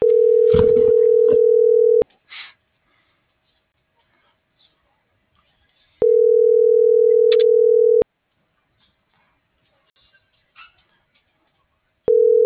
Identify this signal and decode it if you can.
Ringback tone (North America), 440 + 480 Hz.
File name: CENTRAL MAYAB / out raulcristinomayab